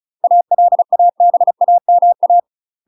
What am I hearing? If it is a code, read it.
Morse code, heard as a keyed tone, about 36 wpm, 700 Hz. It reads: ALABAMA
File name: UK MASTER QSOs TRAINING / 35WPM / UK MASTER QSOs 0007